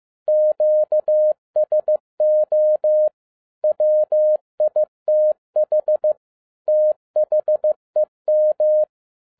Morse code, heard as a keyed tone, about 15 wpm, 610 Hz. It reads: QSO WITH THEM
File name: 2011-10-mp3 / n12-JL3YMV 15wpm